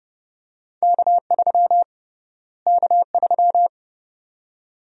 Morse code, heard as a keyed tone, about 30 wpm, 700 Hz. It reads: K3 K3
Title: USA Call Sign Prefixes - Repeated 2x - 30wpm